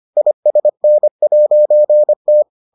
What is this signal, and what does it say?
Morse code, heard as a keyed tone, about 25 wpm, 600 Hz. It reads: ISN'T